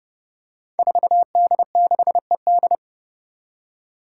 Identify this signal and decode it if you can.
Morse code, heard as a keyed tone, about 30 wpm, 700 Hz. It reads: 4D6ED
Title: Five Challenging Letter-Number Combinations 30wpm